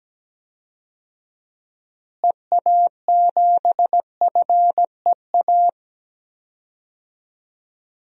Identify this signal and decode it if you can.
Morse code, heard as a keyed tone, about 17 wpm, 700 Hz. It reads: EA7FEA
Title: DXCC-100 Call Signs - 6 Characters - Repeated 3x - 17wpm